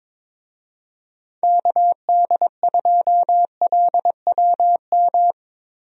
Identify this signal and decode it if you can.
Morse code, heard as a keyed tone, about 22 wpm, 700 Hz. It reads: KD2LWM
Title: USA Call Signs - 6 Characters - Repeated 3x - 22wpm